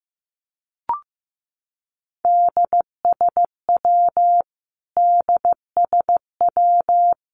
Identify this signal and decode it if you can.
Morse code, heard as a keyed tone, about 15 wpm, 700 Hz. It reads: DSW DSW
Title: Abbreviations Repeated 3x - 15wpm